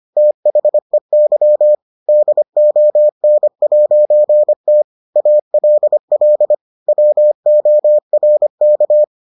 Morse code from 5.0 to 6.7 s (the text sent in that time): ALL